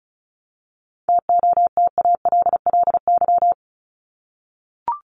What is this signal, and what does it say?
Morse code, heard as a keyed tone, about 34 wpm, 700 Hz. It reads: TOTALLY
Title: LY Suffix - Top 100 Words with Base Words 35wpm